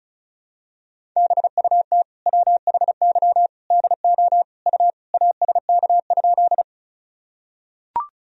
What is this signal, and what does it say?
Morse code, heard as a keyed tone, about 35 wpm, 700 Hz. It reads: BUT WHY DO U ASK?
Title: Sentences from Top 200 Words and Abbreviations 35wpm